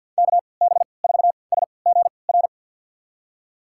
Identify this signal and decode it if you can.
Morse code, heard as a keyed tone, about 51 wpm, spaced as for 28 wpm, 700 Hz. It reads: KB4SGR
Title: USA Call Signs - 3x Character Spacing - 50wpm